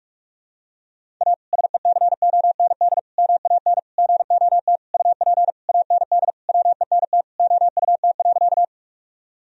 Morse code, heard as a keyed tone, about 46 wpm, 700 Hz. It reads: A SECOND MAN GOT UP AND WENT OUT.
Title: Sentences from Top 1000 Words - Review Part 1 - 45wpm